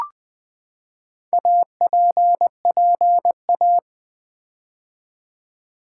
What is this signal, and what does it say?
Morse code, heard as a keyed tone, about 20 wpm, 700 Hz. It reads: APPA
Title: Build A Word - 4 Letters At A Time - 10-Letter Words - 20wpm